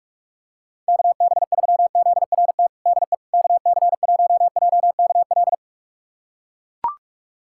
Morse code, heard as a keyed tone, about 41 wpm, 700 Hz. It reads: KB3ZRT DE KC1JKL
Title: USA Callsign DE Callsign 45wpm